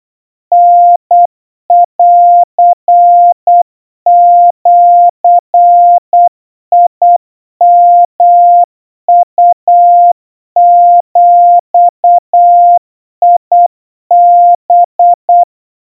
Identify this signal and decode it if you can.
Morse code (Japanese Wabun code), heard as a keyed tone, about 8 wpm, 700 Hz. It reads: タンジヨウビハ
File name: 30 cpm 700 Hz Crispy